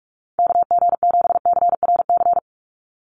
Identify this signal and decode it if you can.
Morse code, heard as a keyed tone, about 46 wpm, 700 Hz. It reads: KG7CRC
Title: USA Call Signs 45wpm